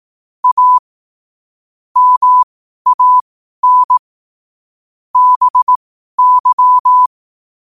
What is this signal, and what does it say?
Morse code, heard as a keyed tone, about 18 wpm, spaced as for 12 wpm, 1000 Hz. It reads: A MAN BY
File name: qotdfiles / 12wpm